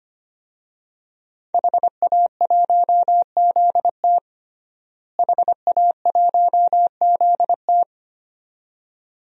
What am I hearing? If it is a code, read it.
Morse code, heard as a keyed tone, about 25 wpm, 700 Hz. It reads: HA1ZT HA1ZT
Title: DXCC-100 Call Signs - 5 Characters - Repeated 2x - 25wpm